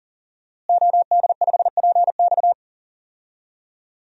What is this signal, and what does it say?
Morse code, heard as a keyed tone, about 39 wpm, 700 Hz. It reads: OD5PX
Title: DXCC-100 Call Signs - 5 Characters - Repeated 3x - 40wpm